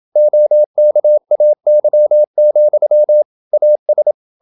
Morse code, heard as a keyed tone, about 28 wpm, 600 Hz. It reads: OKAY, AS
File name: Never 0027wpm